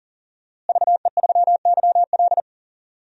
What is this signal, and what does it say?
Morse code, heard as a keyed tone, about 40 wpm, 700 Hz. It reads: VE3YL
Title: DXCC-100 Call Signs - 5 Characters - Repeated 3x - 40wpm